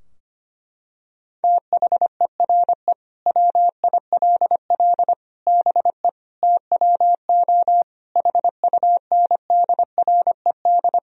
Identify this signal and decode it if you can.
Morse code, heard as a keyed tone, about 25 wpm, 700 Hz. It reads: THERE WILL BE TWO HUNDRED
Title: Sentences from Top 1000 Words - Review Part 3 - 25wpm